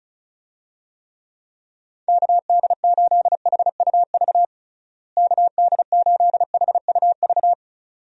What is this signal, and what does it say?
Morse code, heard as a keyed tone, about 34 wpm, 700 Hz. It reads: KD8HUV KD8HUV
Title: USA Call Signs - 6 Characters - Repeated 2x - 35wpm